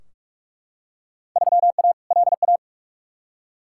Morse code, heard as a keyed tone, about 43 wpm, 700 Hz. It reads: 3A LA
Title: Field Day Exchanges - 45wpm